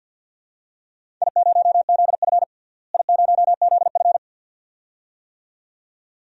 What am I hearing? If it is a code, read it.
Morse code, heard as a keyed tone, about 52 wpm, 700 Hz. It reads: I0ZF I0ZF
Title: DXCC-100 Call Signs - 4 Characters - Repeated 2x - 50wpm